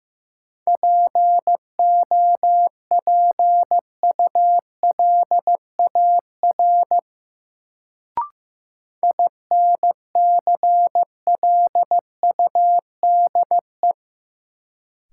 Morse code, heard as a keyed tone, about 15 wpm, 700 Hz. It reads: POPULAR INCLUDE
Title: Sentences from Top 1000 Words - Review Part 3 - New Word Drill - 15wpm